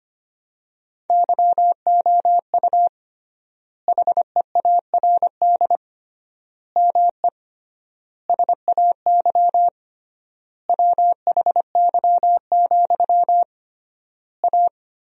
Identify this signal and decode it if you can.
Morse code, heard as a keyed tone, about 25 wpm, 700 Hz. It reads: YOU HEARD ME SAY WHY, A
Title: 3x Word-Spacing - Sentences from Top 500 Words - 25wpm